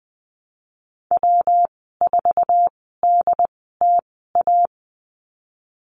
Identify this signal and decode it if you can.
Morse code, heard as a keyed tone, about 20 wpm, spaced as for 15 wpm, 700 Hz. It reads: W4DTA
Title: 122 Lesson 20g - Number 3 - Call Signs 20wpm